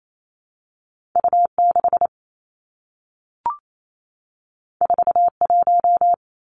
Morse code, heard as a keyed tone, about 28 wpm, 700 Hz. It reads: U 6 4 1
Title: Two Challenging Letter-Number Combinations 28wpm